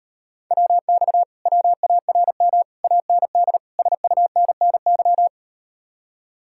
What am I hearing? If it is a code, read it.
Morse code, heard as a keyed tone, about 39 wpm, 700 Hz. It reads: WX WARM AND SUNNY